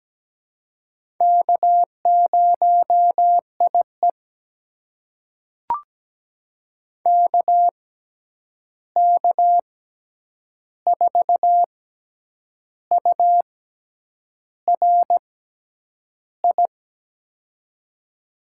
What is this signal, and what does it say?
Morse code, heard as a keyed tone, about 17 wpm, spaced as for 8 wpm, 700 Hz. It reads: K0IE KK4URI
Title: USA Call Signs - 6x Character Spacing - 17wpm